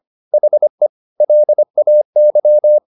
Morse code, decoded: HE LAY

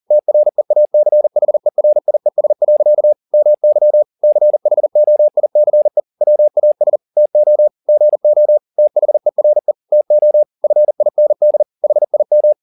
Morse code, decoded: TREACHERIES. MY CHOICE WAS TO GO THERE TO FIND HIM

